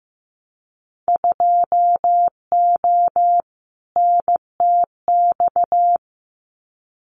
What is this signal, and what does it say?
Morse code, heard as a keyed tone, about 15 wpm, 700 Hz. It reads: 2O NTX
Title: Winter Field Day Exchanges 15wpm